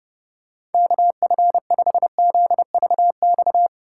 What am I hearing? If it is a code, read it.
Morse code, heard as a keyed tone, about 30 wpm, 700 Hz. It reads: KF5ZVX
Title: USA Call Signs - 8x Character Spacing - 30wpm